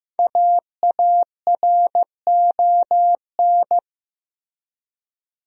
Morse code, heard as a keyed tone, about 15 wpm, 700 Hz. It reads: AARON
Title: USA Common Names - 15wpm